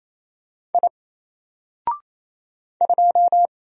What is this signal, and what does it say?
Morse code, heard as a keyed tone, about 27 wpm, 700 Hz. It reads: I 2
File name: Single-Letter-Number-28wpm